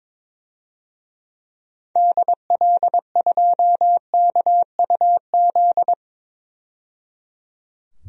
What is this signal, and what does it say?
Morse code, heard as a keyed tone, about 22 wpm, 700 Hz. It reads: DL2KUZ
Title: DXCC-100 Call Signs - 6 Characters - 22wpm